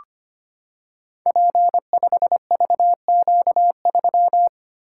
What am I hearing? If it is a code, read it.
Morse code, heard as a keyed tone, about 25 wpm, 700 Hz. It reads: P5VQ3
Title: Code Groups - 5 Letter:Numbers - 25wpm